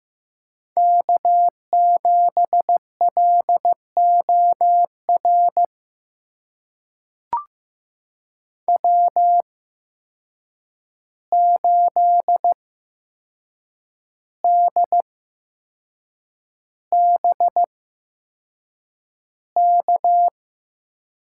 Morse code, heard as a keyed tone, about 15 wpm, spaced as for 6 wpm, 700 Hz. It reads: K7LOR W8DBK
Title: USA Call Signs - 5 Characters - 8x Character Spacing - 15wpm